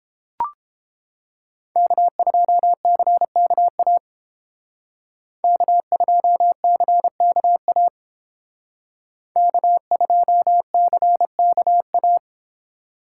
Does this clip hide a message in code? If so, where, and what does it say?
Yes: Morse code from 1.8 to 12.2 s, heard as a keyed tone, about 30 wpm, 700 Hz. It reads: K2CKA K2CKA K2CKA